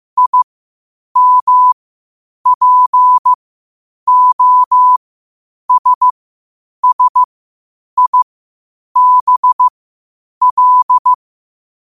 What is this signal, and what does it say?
Morse code, heard as a keyed tone, about 15 wpm, spaced as for 8 wpm, 1000 Hz. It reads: IMPOSSIBL
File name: qotdfiles / 8wpm